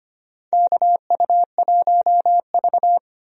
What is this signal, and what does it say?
Morse code, heard as a keyed tone, about 25 wpm, 700 Hz. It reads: KU1V
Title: USA Call Signs - 4 Characters - Repeated 3x - 25wpm